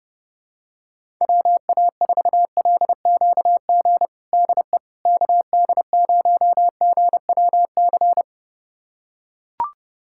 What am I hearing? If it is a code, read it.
Morse code, heard as a keyed tone, about 30 wpm, 700 Hz. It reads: WA4LQG DE KD0GWC
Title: USA Callsign DE Callsign - 6 Characters - 30wpm